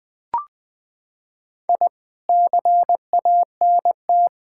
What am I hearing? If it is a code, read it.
Morse code, heard as a keyed tone, about 19 wpm, 700 Hz. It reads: I CANT
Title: Sets of 8 Words - 20wpm